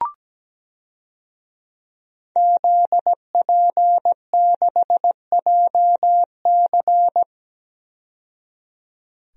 Morse code, heard as a keyed tone, about 17 wpm, 700 Hz. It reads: ZP6JC
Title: DXCC-100 Call Signs - 5 Characters - 17wpm